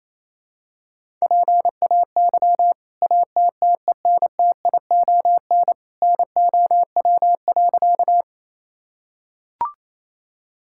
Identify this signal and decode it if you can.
Morse code, heard as a keyed tone, about 28 wpm, 700 Hz. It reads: PAY ATTENTION NOW.
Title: Sentences from Top 1000 Words - Review Part 6 - 28wpm